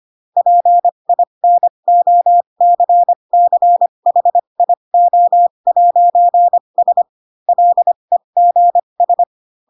Morse code, heard as a keyed tone, about 25 wpm, spaced as for 22 wpm, 700 Hz. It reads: PINOCCHIO'S LEGS
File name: Pinocchio / Pinocchio ch3 18 25 ik1hge